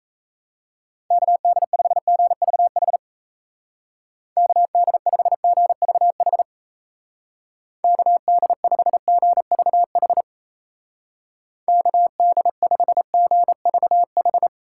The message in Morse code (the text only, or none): KD5GVH KD5GVH KD5GVH KD5GVH